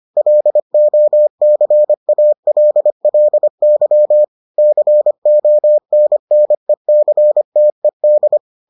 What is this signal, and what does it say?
Morse code, heard as a keyed tone, about 25 wpm, 600 Hz. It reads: LOCALLY CONNECTED